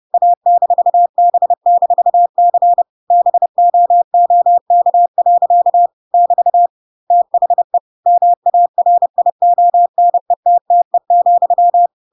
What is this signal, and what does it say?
Morse code, heard as a keyed tone, about 30 wpm, 700 Hz. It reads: A-B-C BOOK. <BT> THE MARIONETTE,
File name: Pinocchio / Pinocchio ch8 30 30 ik1hge